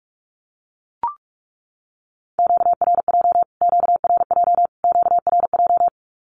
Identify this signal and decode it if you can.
Morse code, heard as a keyed tone, about 44 wpm, 700 Hz. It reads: QRJ QRJ QRJ
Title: Q-Codes Repeated 3x 45wpm